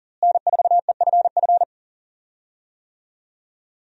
Morse code, heard as a keyed tone, about 40 wpm, 700 Hz. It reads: N4EFF
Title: USA Call Signs - 5 Characters - Repeated 3x - 40wpm